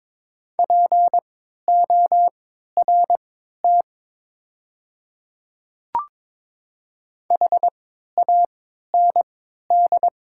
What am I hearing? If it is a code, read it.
Morse code, heard as a keyed tone, about 22 wpm, spaced as for 13 wpm, 700 Hz. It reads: PORT HAND
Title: 3x Character Spacing - Top 200 Words - 22wpm